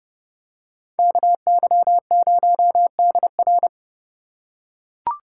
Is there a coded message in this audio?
Yes: Morse code from 1.0 to 3.7 s, heard as a keyed tone, about 29 wpm, 700 Hz. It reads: KY0DR